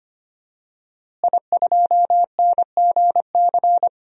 Morse code, heard as a keyed tone, about 25 wpm, 700 Hz. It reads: I2NGC
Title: DXCC-100 Call Signs - 5 Characters - Repeated 3x - 25wpm